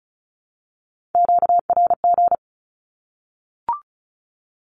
Morse code, heard as a keyed tone, about 34 wpm, 700 Hz. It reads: QRG
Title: Q-Codes 35wpm